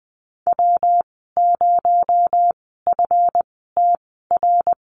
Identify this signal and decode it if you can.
Morse code, heard as a keyed tone, about 20 wpm, spaced as for 14 wpm, 700 Hz. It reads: W0FTR
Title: 245 Lesson 39g - Letter z - Call Signs 20wpm